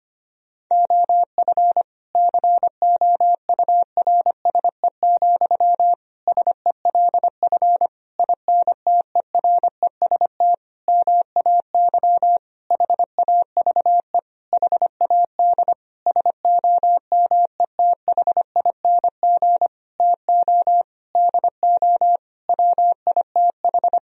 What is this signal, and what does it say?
Morse code, heard as a keyed tone, about 25 wpm, 700 Hz. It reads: OF COURSE, SELF INTEREST MAY HAVE HAD SOMETHING TO DO WITH
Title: Sentences from Top 1000 Words - Review Part 2 - 25wpm